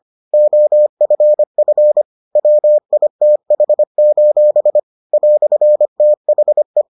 Morse code, heard as a keyed tone, about 25 wpm, 600 Hz. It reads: OFF WITH: "THE